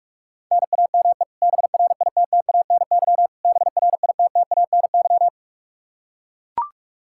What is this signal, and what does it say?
Morse code, heard as a keyed tone, about 44 wpm, 700 Hz. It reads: NAME BRITTANY BRITTANY
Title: Practice QSO - Exchange 1 - Beginning - 45wpm